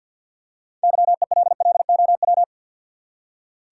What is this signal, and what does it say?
Morse code, heard as a keyed tone, about 48 wpm, 700 Hz. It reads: YELLOW
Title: Six Letter Words - 50wpm